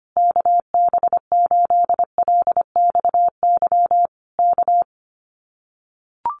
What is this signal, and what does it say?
Morse code, heard as a keyed tone, about 25 wpm, 700 Hz. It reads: KB8LXY K
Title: Common QSO Phrases Encore 25wpm